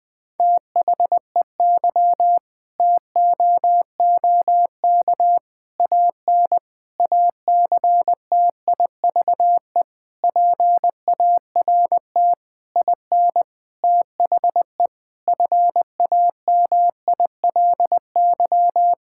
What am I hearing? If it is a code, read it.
Morse code, heard as a keyed tone, about 20 wpm, 700 Hz. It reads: THEY TOOK AN ACTIVE PART IN THE FAMILY
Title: Sentences from Top 1000 Words - Review Part 6 - 20wpm